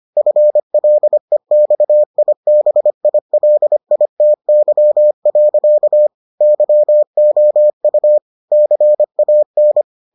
Morse code, heard as a keyed tone, about 25 wpm, 600 Hz. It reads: FLEXIBILITY. YOU CAN